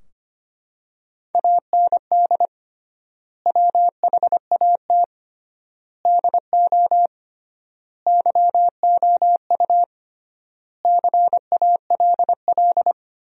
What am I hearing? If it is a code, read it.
Morse code, heard as a keyed tone, about 25 wpm, 700 Hz. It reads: AND WHAT DO YOU CALL
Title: 3x Word-Spacing - Sentences from Top 100 Words - 25wpm